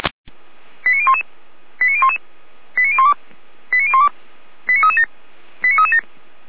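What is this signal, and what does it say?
Five-tone selective call (ZVEI), heard as quick runs of five tones, tones of about 70 ms. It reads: 89011 89011 89012 89012 89337 89337
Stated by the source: ZVEI five-tone selective call 89174 received on an SDR.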